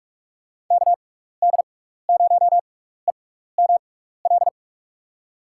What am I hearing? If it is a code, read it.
Morse code, heard as a keyed tone, about 44 wpm, spaced as for 15 wpm, 700 Hz. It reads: KD0EML
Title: USA Call Signs - 6 Characters - 6x Character Spacing - 45wpm